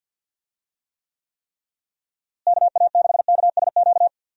Morse code, heard as a keyed tone, about 49 wpm, 700 Hz. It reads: KA6KSQ
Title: USA Call Signs - 6 Characters - Repeated 3x - 50wpm